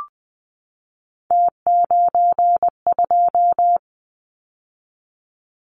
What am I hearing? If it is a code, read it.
Morse code, heard as a keyed tone, about 20 wpm, 700 Hz. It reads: T92
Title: Callsign Prefixes for Top 50 DXCC - 20wpm